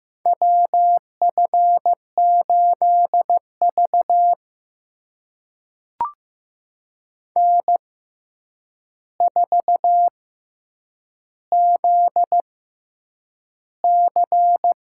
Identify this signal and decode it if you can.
Morse code, heard as a keyed tone, about 15 wpm, spaced as for 8 wpm, 700 Hz. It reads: WF8V N4ZC
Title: USA Call Signs - 4 Characters - 6x Character Spacing - 15wpm